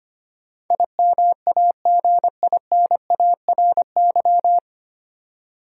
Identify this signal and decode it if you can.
Morse code, heard as a keyed tone, about 25 wpm, 700 Hz. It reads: IMAGINARY
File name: Nine Letter Words - 25wpm